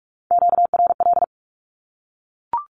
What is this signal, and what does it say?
Morse code, heard as a keyed tone, about 44 wpm, 700 Hz. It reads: QRL
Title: Q-Codes 45wpm